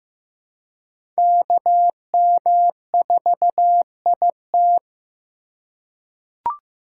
Morse code, heard as a keyed tone, about 15 wpm, 700 Hz. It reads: KM4IT
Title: USA Call Signs - 3x Character Spacing - 15wpm